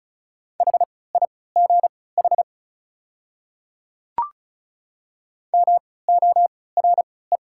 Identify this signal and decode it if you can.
Morse code, heard as a keyed tone, about 36 wpm, spaced as for 20 wpm, 700 Hz. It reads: HIGH MORE
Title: 3x Character Spacing - Top 300 Words - 35wpm